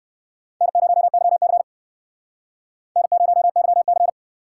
Morse code, heard as a keyed tone, about 59 wpm, 700 Hz. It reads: N0YC N0YC